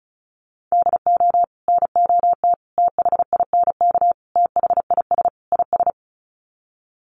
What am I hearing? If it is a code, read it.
Morse code, heard as a keyed tone, about 34 wpm, 700 Hz. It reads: DO NOT THINK THIS IS
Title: Sets of 5 Words Encore2 35wpm